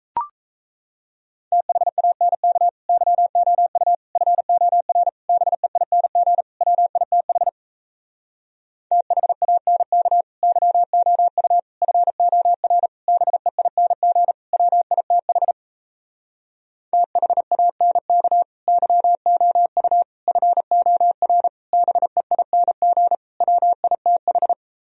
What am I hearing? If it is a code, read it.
Morse code, heard as a keyed tone, about 37 wpm, 700 Hz. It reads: THANK YOU FOR BEING WITH THANK YOU FOR BEING WITH THANK YOU FOR BEING WITH